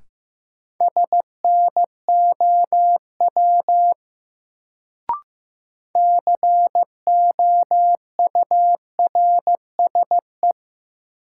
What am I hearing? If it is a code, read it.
Morse code, heard as a keyed tone, about 15 wpm, 700 Hz. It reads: SNOW COURSE
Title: Sentences from Top 500 Words - New Word Drill - 15wpm